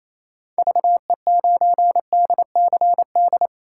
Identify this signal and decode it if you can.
Morse code, heard as a keyed tone, about 28 wpm, 700 Hz. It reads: VE9DCD
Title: K1USN SST QSO Practice - 28wpm